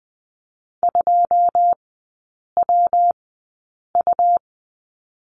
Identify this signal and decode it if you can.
Morse code, heard as a keyed tone, about 20 wpm, spaced as for 8 wpm, 700 Hz. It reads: 2WU
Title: 114 Lesson 19e - Letter w - Three Characters 20wpm